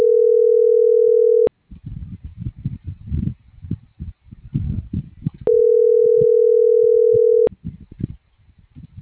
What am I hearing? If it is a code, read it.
Ringback tone (North America), 440 + 480 Hz.